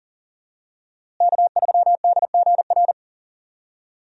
Morse code, heard as a keyed tone, about 37 wpm, 700 Hz. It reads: K3DGR
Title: USA Call Signs - 5 Characters - Repeated 3x - 40wpm